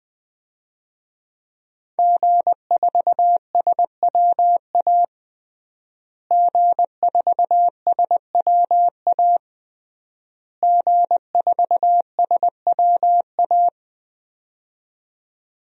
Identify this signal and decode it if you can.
Morse code, heard as a keyed tone, about 20 wpm, 700 Hz. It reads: G4SWA G4SWA G4SWA
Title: DXCC-100 Call Signs - 5 Characters - Repeated 3x - 20wpm